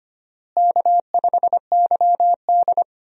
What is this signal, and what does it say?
Morse code, heard as a keyed tone, about 25 wpm, 700 Hz. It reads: K5YD
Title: USA Call Signs - 4 Characters - Repeated 3x - 25wpm